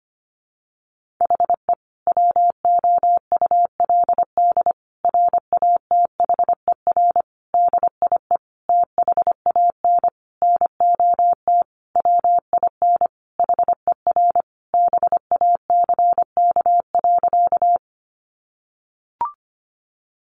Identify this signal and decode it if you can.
Morse code, heard as a keyed tone, about 25 wpm, 700 Hz. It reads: HE WOULD RATHER DIE THAN NOT WIN HER BACK.
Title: Sentences from Top 1500 Words 25wpm